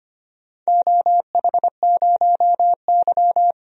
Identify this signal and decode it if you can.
Morse code, heard as a keyed tone, about 25 wpm, 700 Hz. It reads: OH0Y
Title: DXCC-100 Call Signs - 4 Characters - 25wpm